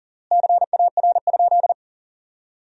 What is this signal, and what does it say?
Morse code, heard as a keyed tone, about 40 wpm, 700 Hz. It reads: CAR?
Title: 3x Word-Spacing - Sentences from Top 300 Words - 40wpm